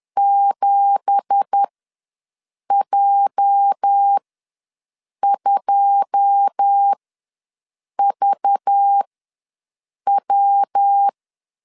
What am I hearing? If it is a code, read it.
Morse code, heard as a keyed tone, about 10 wpm, spaced as for 6 wpm, 800 Hz. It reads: 7J2VW
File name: LEZIONI / 10 SIGNS / 10C DOT ws 10wpm 4213